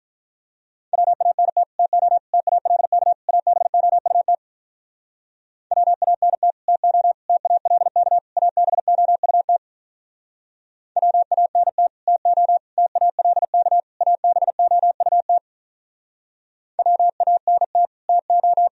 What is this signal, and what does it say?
Morse code, heard as a keyed tone, about 45 wpm, 700 Hz. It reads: WANT TO TALK ABOUT WANT TO TALK ABOUT WANT TO TALK ABOUT WANT TO